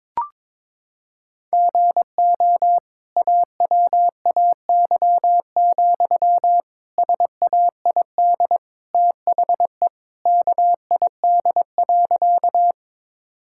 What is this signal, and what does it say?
Morse code, heard as a keyed tone, about 22 wpm, 700 Hz. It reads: GO AWAY, SAID THE KID.
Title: Sentences from Top 1600 Words 22wpm